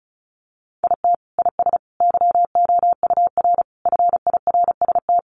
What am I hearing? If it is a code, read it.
Morse code, heard as a keyed tone, about 34 wpm, 700 Hz. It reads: IT IS YOUR FIRST
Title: Sentences from Top 1100 Words 35wpm